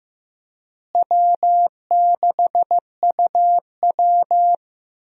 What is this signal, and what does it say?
Morse code, heard as a keyed tone, about 15 wpm, 700 Hz. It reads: W6UW
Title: USA Call Signs - 4 Characters - Repeated 3x - 15wpm